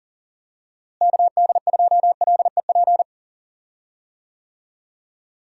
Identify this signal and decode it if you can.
Morse code, heard as a keyed tone, about 40 wpm, 700 Hz. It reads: KD2LEP
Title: USA Call Signs - 6 Characters - Repeated 3x - 40wpm